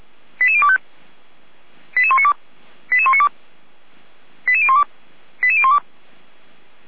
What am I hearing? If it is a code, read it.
Five-tone selective call (ZVEI), heard as quick runs of five tones, tones of about 70 ms. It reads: 88025 88192 88192 88012 88012